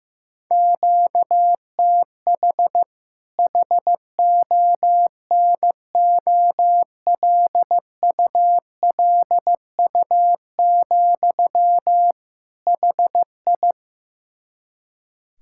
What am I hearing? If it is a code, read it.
Morse code, heard as a keyed tone, about 15 wpm, 700 Hz. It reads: QTH HONOLULU, HI
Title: K1USN SST QSO Practice 15wpm